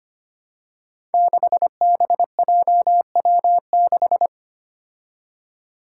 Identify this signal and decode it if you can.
Morse code, heard as a keyed tone, about 25 wpm, 700 Hz. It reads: 6BJW6
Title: Five Challenging Letter-Number Combinations 25wpm